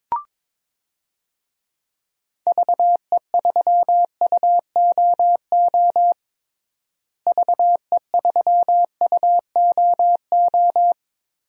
Morse code, heard as a keyed tone, about 22 wpm, 700 Hz. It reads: VE3UOO VE3UOO
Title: DXCC-100 Call Signs - 6 Characters - Repeated 3x - 22wpm